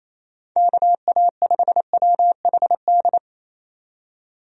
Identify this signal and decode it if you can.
Morse code, heard as a keyed tone, about 28 wpm, 700 Hz. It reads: KA5WHD
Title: USA Call Signs - 6 Characters - 28wpm